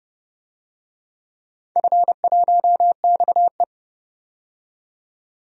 Morse code, heard as a keyed tone, about 29 wpm, 700 Hz. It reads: F1XE
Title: DXCC-100 Call Signs - 4 Characters - 30wpm